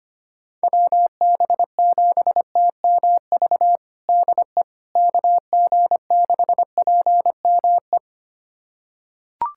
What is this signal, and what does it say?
Morse code, heard as a keyed tone, about 25 wpm, 700 Hz. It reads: WB7TMV DE KG6PME
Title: USA Callsign DE Callsign 25wpm